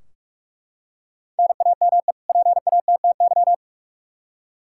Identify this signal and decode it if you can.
Morse code, heard as a keyed tone, about 41 wpm, 700 Hz. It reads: NAME PATTY
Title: K1USN SST QSO Practice 45wpm